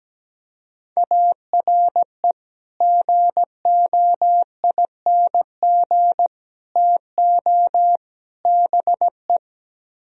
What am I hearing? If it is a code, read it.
Morse code, heard as a keyed tone, about 17 wpm, 700 Hz. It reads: ARE GOING TO BE
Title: Speed-Racing - Sets of 4 Words 17wpm